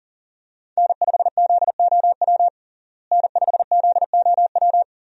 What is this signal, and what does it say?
Morse code, heard as a keyed tone, about 40 wpm, 700 Hz. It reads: N5ZOW N5ZOW
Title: USA Call Signs - 5 Characters - Repeated 2x - 40wpm